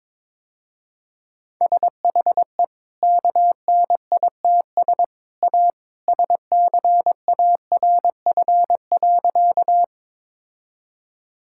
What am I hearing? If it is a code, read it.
Morse code, heard as a keyed tone, about 22 wpm, 700 Hz. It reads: SHE KNITS A SCARF.
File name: practice / 22wpm / Easy Sentences with Hints - 22wpm